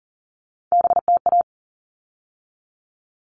Morse code, heard as a keyed tone, about 40 wpm, 700 Hz. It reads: BTA
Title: All Cut Numbers - 3-Digit 40wpm